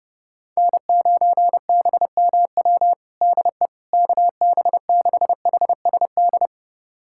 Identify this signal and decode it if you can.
Morse code, heard as a keyed tone, about 30 wpm, 700 Hz. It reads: N9BMW DE KB6HSD
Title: USA Callsign DE Callsign 30wpm